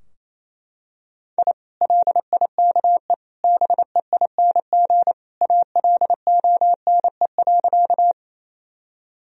Morse code, heard as a keyed tone, about 28 wpm, 700 Hz. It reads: I LIKE BEING ALONE.
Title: Sentences from Top 1000 Words - Review Part 3 - 28wpm